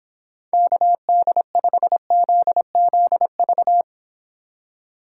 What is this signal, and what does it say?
Morse code, heard as a keyed tone, about 26 wpm, 700 Hz. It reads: KD5ZZV